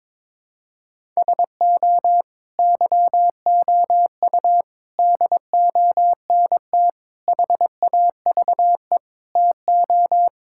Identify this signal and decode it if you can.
Morse code, heard as a keyed tone, about 22 wpm, 700 Hz. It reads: SO YOU DONT HAVE TO
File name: practice / 22wpm / Sets of 7 Words - 22wpm